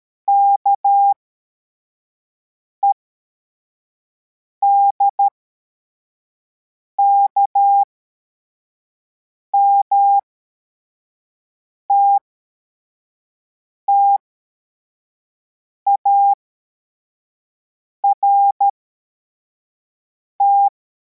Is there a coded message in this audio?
Yes: Morse code from 0.3 to 20.7 s, heard as a keyed tone, about 13 wpm, spaced as for 4 wpm, 800 Hz. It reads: KEDKM TTART